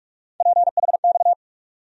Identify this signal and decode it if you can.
Morse code, heard as a keyed tone, about 44 wpm, 700 Hz. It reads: PHX
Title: USA Call Sign Postfixes - 3 Letters 45wpm